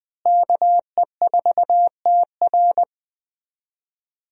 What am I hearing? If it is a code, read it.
Morse code, heard as a keyed tone, about 20 wpm, 700 Hz. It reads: KE4TR
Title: USA Call Signs - 5 Characters - Repeated 3x - 20wpm